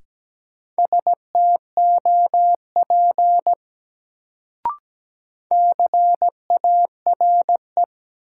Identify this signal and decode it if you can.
Morse code, heard as a keyed tone, about 17 wpm, 700 Hz. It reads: STOP CARE
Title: Sentences from Top 300 Words - New Word Drill - 17wpm